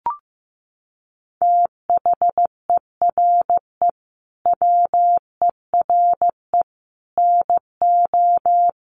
Morse code, heard as a keyed tone, about 15 wpm, 700 Hz. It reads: THERE WERE NO
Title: Sets of 5 Words Encore2 15wpm